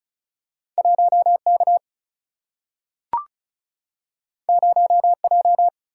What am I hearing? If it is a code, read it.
Morse code, heard as a keyed tone, about 36 wpm, 700 Hz. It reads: 1 K 0 J
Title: Single-Letter Single-Number 35wpm